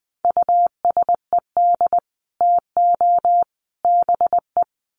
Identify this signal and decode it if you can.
Morse code, heard as a keyed tone, about 20 wpm, 700 Hz. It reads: USED TO BE
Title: Sets of 3 Words 20wpm